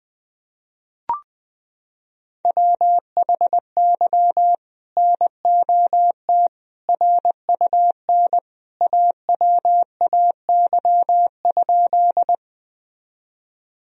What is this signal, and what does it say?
Morse code, heard as a keyed tone, about 21 wpm, 700 Hz. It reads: WHY NOT RUN AWAY?
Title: Sentences from Top 1000 Words - Review Part 1 - 20wpm